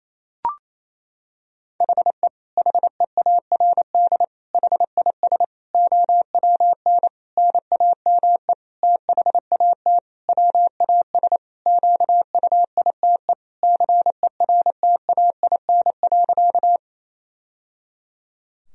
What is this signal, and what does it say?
Morse code, heard as a keyed tone, about 28 wpm, 700 Hz. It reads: HE HEARD HIS OWN NAME THAT WAS QUITE CERTAIN.
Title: Sentences from Top 1000 Words - Review Part 2 - 28wpm